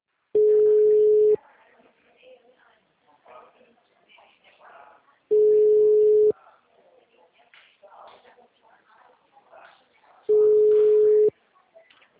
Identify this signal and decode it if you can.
Ringback tone (Europe, 425 Hz), with speech.